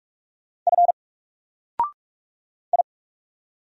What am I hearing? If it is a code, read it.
Morse code, heard as a keyed tone, about 44 wpm, 700 Hz. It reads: F I
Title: Single Character (Letter:Number:Punctuation:Prosign) Alt 45wpm